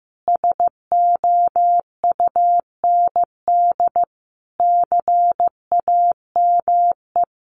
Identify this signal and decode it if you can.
Morse code, heard as a keyed tone, about 15 wpm, 700 Hz. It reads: SOUND CAME
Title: Sentences from Top 1400 Words 15wpm